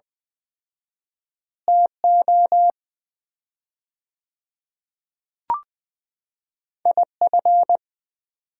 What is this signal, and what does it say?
Morse code, heard as a keyed tone, about 20 wpm, 700 Hz. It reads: TO IF
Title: 4x Word-Spacing - Sets of 2 Words - 20wpm